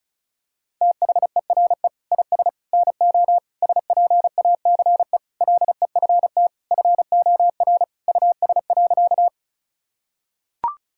Morse code, heard as a keyed tone, about 36 wpm, 700 Hz. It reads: THERE IS NO SPACE LEFT FOR US.
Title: Sentences from Top 1000 Words - Review Part 5 - 35wpm